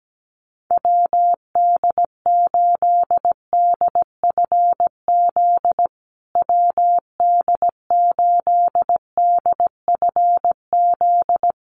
Morse code, heard as a keyed tone, about 17 wpm, 700 Hz. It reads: WD8DFZ WD8DFZ
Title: USA Call Signs Repeated 2x 17wpm